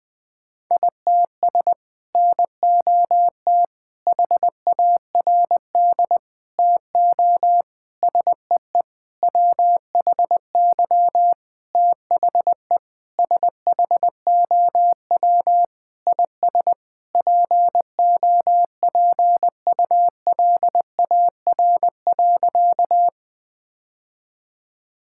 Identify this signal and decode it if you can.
Morse code, heard as a keyed tone, about 20 wpm, 700 Hz. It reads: ITS NOT HARD TO SEE WHY THE SHOW IS POPULAR.
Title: Sentences with 6-Word Phrases - 20wpm